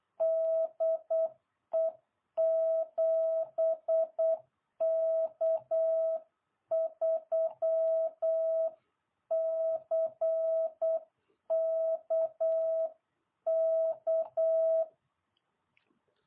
Morse code, heard as a keyed tone, about 8 wpm, 660 Hz. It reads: DE 7K3CKK